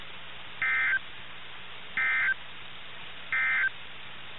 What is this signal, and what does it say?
Emergency Alert System (SAME) tones, heard as harsh buzzing bursts. It is heard from 0.5 to 3.8 s.
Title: NOAA Weather Radio - KIH61